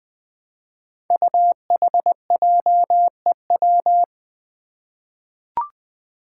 Morse code, heard as a keyed tone, about 20 wpm, 700 Hz. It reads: UHJEW
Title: Five Challenging Letter-Number Combinations 20wpm